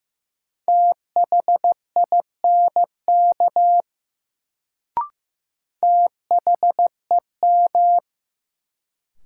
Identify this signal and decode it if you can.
Morse code, heard as a keyed tone, about 15 wpm, 700 Hz. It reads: THINK THEM
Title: Sets of 2 Words - New Word Drill - 15wpm